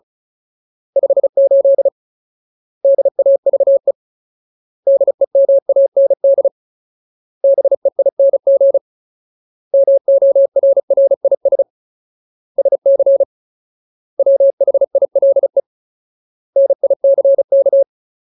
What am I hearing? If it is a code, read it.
Morse code, heard as a keyed tone, about 34 wpm, 550 Hz. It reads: 58 DAVE DEMAND BEING MORRIS SC WHILE NICK
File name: Ultimate Word List / 35WPM Ultimate